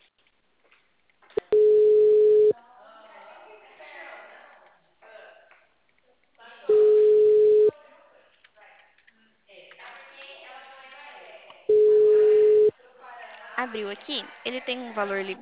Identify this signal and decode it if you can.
Ringback tone (Europe, 425 Hz), with speech.
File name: out- -2044- - - .3602